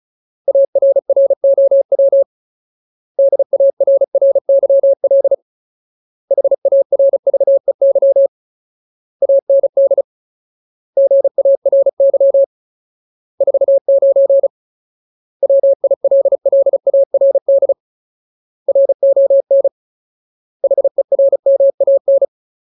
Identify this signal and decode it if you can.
Morse code, heard as a keyed tone, about 34 wpm, 550 Hz. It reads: ARROW DARRYL HARVEY AND GARY 49 WILLARD RON HERMAN
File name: Ultimate Word List / 35WPM Ultimate